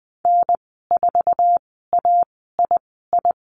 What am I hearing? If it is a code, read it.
Morse code, heard as a keyed tone, about 20 wpm, spaced as for 14 wpm, 700 Hz. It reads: N4AII
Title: 048 Lesson 9g - Number 4 - Call Signs 20wpm